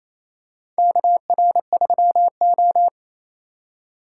Morse code, heard as a keyed tone, about 27 wpm, 700 Hz. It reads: KR3O
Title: USA Call Signs - 4 Characters - 6x Character Spacing - 28wpm